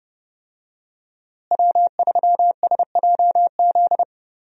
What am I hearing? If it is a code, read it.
Morse code, heard as a keyed tone, about 31 wpm, 700 Hz. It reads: W3SJZ
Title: USA Call Signs - 5 Characters - Repeated 2x - 30wpm